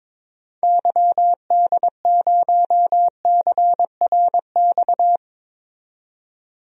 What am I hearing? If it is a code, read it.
Morse code, heard as a keyed tone, about 21 wpm, 700 Hz. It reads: YD0CRX
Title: DXCC-100 Call Signs - 6 Characters - 22wpm